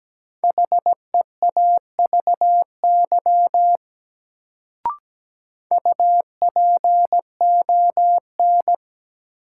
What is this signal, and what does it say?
Morse code, heard as a keyed tone, about 17 wpm, 700 Hz. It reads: HEAVY UPON
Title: Sentences from Top 1000 Words - Review Part 6 - New Word Drill - 17wpm